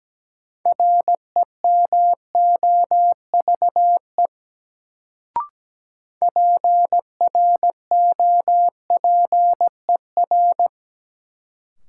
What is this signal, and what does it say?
Morse code, heard as a keyed tone, about 17 wpm, 700 Hz. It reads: REMOVE PROPER
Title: Sentences from Top 800 Words - New Word Drill - 17wpm